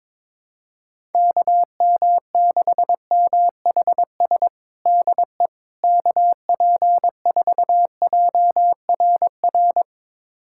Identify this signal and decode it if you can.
Morse code, heard as a keyed tone, about 22 wpm, 700 Hz. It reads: KM6MHS DE KP4JRR
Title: USA Callsign DE Callsign - 6 Characters - 22wpm